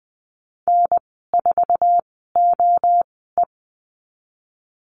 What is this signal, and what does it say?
Morse code, heard as a keyed tone, about 19 wpm, spaced as for 14 wpm, 700 Hz. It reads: N4OE
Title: 048 Lesson 9g - Number 4 - Call Signs 20wpm